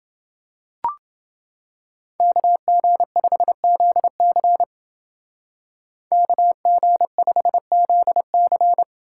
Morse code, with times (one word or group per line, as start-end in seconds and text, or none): 2.2-4.6 KG5ZC
6.1-8.8 KG5ZC